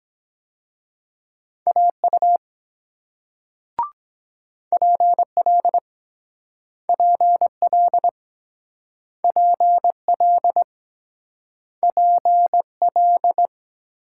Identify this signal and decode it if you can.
Morse code, heard as a keyed tone, about 23 wpm, 700 Hz. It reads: AU PL PL PL PL